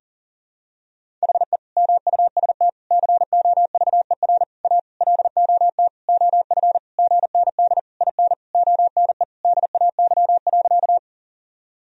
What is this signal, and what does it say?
Morse code, heard as a keyed tone, about 37 wpm, 700 Hz. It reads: HE MUST COVER A LOT OF GND IN ONE DAY.
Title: Sentences from Top 500 Words and Abbreviations 40wpm